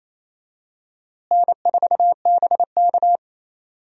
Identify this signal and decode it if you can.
Morse code, heard as a keyed tone, about 28 wpm, 700 Hz. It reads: N4BK
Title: USA Call Signs - 4 Characters - Repeated 2x - 28wpm